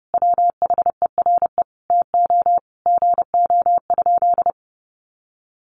Morse code, heard as a keyed tone, about 30 wpm, 700 Hz. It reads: WHERE TO GO?
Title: Sentences from Top 1300 Words 30wpm